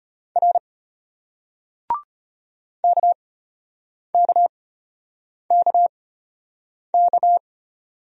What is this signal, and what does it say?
Morse code, heard as a keyed tone, about 33 wpm, spaced as for 8 wpm, 700 Hz. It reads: R KKKK